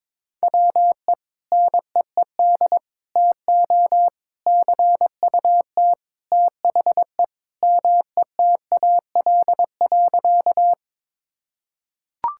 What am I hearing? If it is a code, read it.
Morse code, heard as a keyed tone, about 21 wpm, 700 Hz. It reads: WE NEED TO CUT THE METAL.
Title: Sentences from Top 1000 Words - Review Part 6 - 22wpm